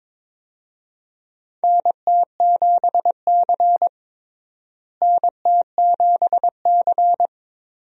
Morse code, heard as a keyed tone, about 22 wpm, 700 Hz. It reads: NT7C NT7C
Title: USA Call Signs - 4 Characters - Repeated 3x - 22wpm